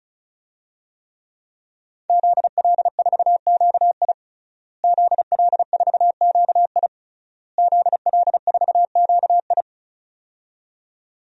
Morse code, heard as a keyed tone, about 34 wpm, 700 Hz. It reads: ZL4QI ZL4QI ZL4QI
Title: DXCC-100 Call Signs - 5 Characters - Repeated 3x - 35wpm